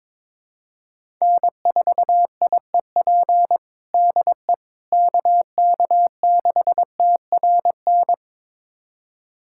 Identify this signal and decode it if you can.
Morse code, heard as a keyed tone, about 22 wpm, 700 Hz. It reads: N4IEP DE KK6TRN
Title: USA Callsign DE Callsign 22wpm